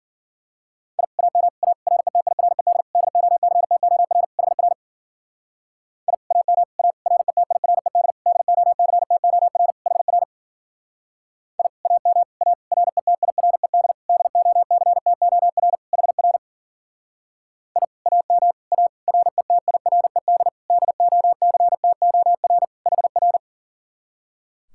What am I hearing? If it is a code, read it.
Morse code, heard as a keyed tone, about 47 wpm, 700 Hz. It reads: I AM A RETIRED DOCTOR HR I AM A RETIRED DOCTOR HR I AM A RETIRED DOCTOR HR I AM A RETIRED DOCTOR HR
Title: Speed-Racing Common QSO Phrases - 40wpm